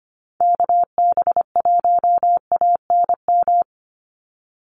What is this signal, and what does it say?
Morse code, heard as a keyed tone, about 25 wpm, 700 Hz. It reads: KB1ANM
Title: USA Call Signs 25wpm